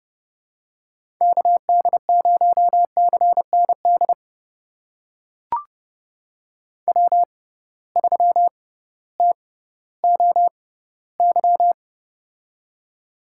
Morse code, heard as a keyed tone, about 30 wpm, spaced as for 16 wpm, 700 Hz. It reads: KD0CND W3TOY
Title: USA Call Signs - 6x Character Spacing - 30wpm